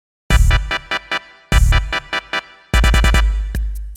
There is no code.